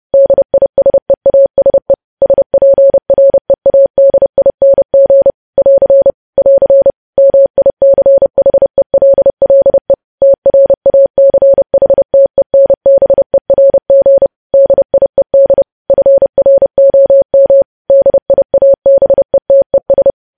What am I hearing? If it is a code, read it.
Morse code, heard as a keyed tone, about 30 wpm, 550 Hz. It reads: DISEASE SPREADING <AR> <AR> MICHELLE TRACHTENBERG DIED FROM DIABETES
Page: news headlines in morse code at 30 wpm